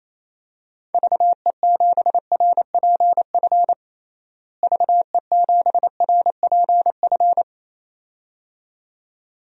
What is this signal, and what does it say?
Morse code, heard as a keyed tone, about 28 wpm, 700 Hz. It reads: VE7RPF VE7RPF
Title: DXCC-100 Call Signs - 6 Characters - Repeated 2x - 28wpm